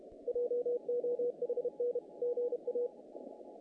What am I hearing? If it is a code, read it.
Morse code, heard as a keyed tone, about 30 wpm, 480 Hz. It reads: JOHN GA
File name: contests / naqp-cw-k5tr-op-k5pi- -2100